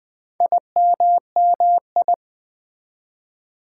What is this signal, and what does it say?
Morse code, heard as a keyed tone, about 20 wpm, 700 Hz. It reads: IMMI
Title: Build A Word - 2 Letters At A Time - 8-Letter Words - 20wpm